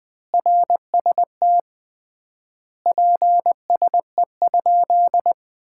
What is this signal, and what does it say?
Morse code, heard as a keyed tone, about 20 wpm, 700 Hz. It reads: RST PSE?
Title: Common QSO Phrases - 3x Word-Spacing - 20wpm